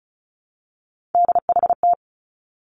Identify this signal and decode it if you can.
Morse code, heard as a keyed tone, about 36 wpm, 700 Hz. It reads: DHT
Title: Three Challenging Letter-Number Combinations 35wpm